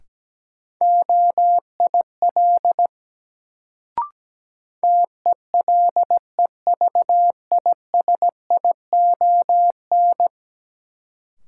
Morse code, heard as a keyed tone, about 17 wpm, 700 Hz. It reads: OIL TELEVISION
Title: Sentences from Top 1000 Words - Review Part 4 - New Word Drill - 17wpm